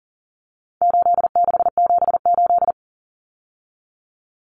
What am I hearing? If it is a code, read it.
Morse code, heard as a keyed tone, about 40 wpm, 700 Hz. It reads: 8678
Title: Numbers - 4-Digit 40wpm